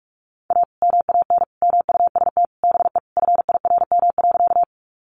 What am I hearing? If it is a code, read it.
Morse code, heard as a keyed tone, about 44 wpm, 700 Hz. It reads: A MAN MUST BE FIRM.
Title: Sentences from Top 1400 Words 45wpm